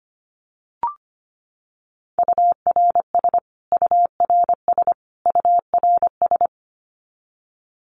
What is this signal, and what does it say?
Morse code, heard as a keyed tone, about 25 wpm, 700 Hz. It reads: URS URS URS
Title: Abbreviations Repeated 3x - 25wpm